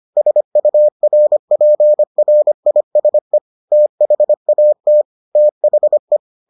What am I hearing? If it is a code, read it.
Morse code, heard as a keyed tone, about 25 wpm, 600 Hz. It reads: SURPRISE THAT THE